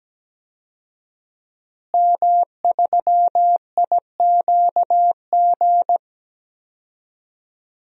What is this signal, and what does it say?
Morse code, heard as a keyed tone, about 17 wpm, 700 Hz. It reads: M3IQG
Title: DXCC-100 Call Signs - 5 Characters - Repeated 2x - 17wpm